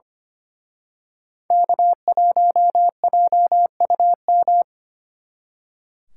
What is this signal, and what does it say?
Morse code, heard as a keyed tone, about 31 wpm, 700 Hz. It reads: K1JUM
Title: Speed Racing - USA 1x3 Call Signs 25wpm